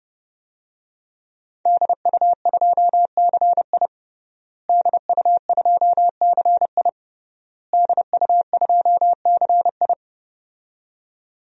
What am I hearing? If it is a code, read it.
Morse code, heard as a keyed tone, about 30 wpm, 700 Hz. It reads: DU2CI DU2CI DU2CI
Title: DXCC-100 Call Signs - 5 Characters - Repeated 3x - 30wpm